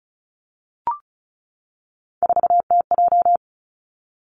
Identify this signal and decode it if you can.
Morse code, heard as a keyed tone, about 34 wpm, 700 Hz. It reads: 4TJ
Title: Three Challenging Letter-Number Combinations 35wpm